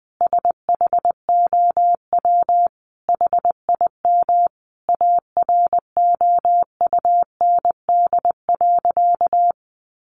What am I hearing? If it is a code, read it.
Morse code, heard as a keyed tone, about 20 wpm, 700 Hz. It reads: SHOW HIM AROUND.
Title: Easy Phrases 20wpm